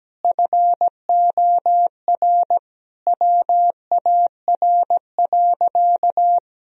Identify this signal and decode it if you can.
Morse code, heard as a keyed tone, about 17 wpm, 700 Hz. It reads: FOR WAR.
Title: Sentences from Top 1000 Words - Review Part 3 - 17wpm